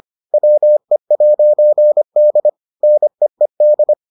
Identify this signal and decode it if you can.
Morse code, heard as a keyed tone, about 25 wpm, 600 Hz. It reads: WE'D NEED